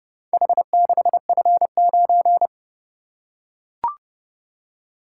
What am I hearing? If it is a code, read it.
Morse code, heard as a keyed tone, about 30 wpm, 700 Hz. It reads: H6F9